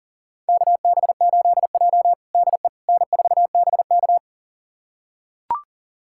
Morse code, heard as a keyed tone, about 37 wpm, 700 Hz. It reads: KB8J DE N4BK
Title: USA Callsign DE Callsign - 4 Characters - 40wpm